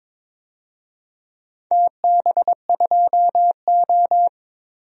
Morse code, heard as a keyed tone, about 22 wpm, 700 Hz. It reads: TB2O
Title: DXCC-100 Call Signs - 4 Characters - 22wpm